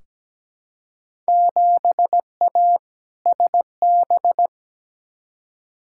Morse code, heard as a keyed tone, about 17 wpm, 700 Hz. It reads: 7A SB
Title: Field Day Exchanges Encore - 17wpm